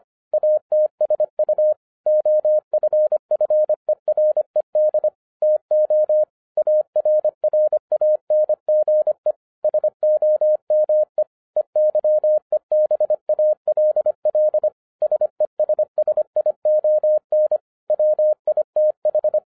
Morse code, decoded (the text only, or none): ATSU OFFERED TO ARRANGE SOME EYEBALL SESSION WITH